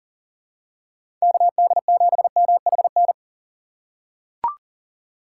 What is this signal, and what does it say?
Morse code, heard as a keyed tone, about 40 wpm, 700 Hz. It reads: KD7MHN
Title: USA Call Signs - 2x Character Spacing - 40wpm